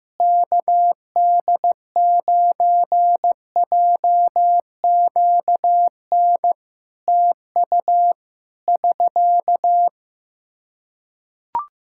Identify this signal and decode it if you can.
Morse code, heard as a keyed tone, about 15 wpm, 700 Hz. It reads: KD9JQN TU <SK>
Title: K1USN SST QSO Practice 15wpm